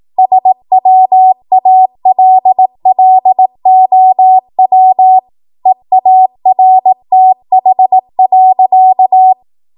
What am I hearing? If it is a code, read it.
Morse code, heard as a keyed tone, about 18 wpm, 750 Hz. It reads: SWALLOW EARTH.
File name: Morse / 18WPM